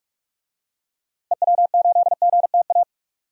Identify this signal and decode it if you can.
Morse code, heard as a keyed tone, about 44 wpm, 700 Hz. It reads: EW8GTA
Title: DXCC-100 Call Signs - 6 Characters - Repeated 3x - 45wpm